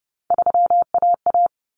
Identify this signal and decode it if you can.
Morse code, heard as a keyed tone, about 30 wpm, 700 Hz. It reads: 3AA
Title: Common Cut Numbers - 3-Digit 30wpm